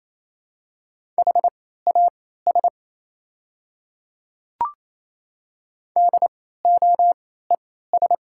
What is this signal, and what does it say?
Morse code, heard as a keyed tone, about 27 wpm, spaced as for 16 wpm, 690 Hz. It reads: HAS DOES
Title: 3x Character Spacing - Top 200 Words - 28wpm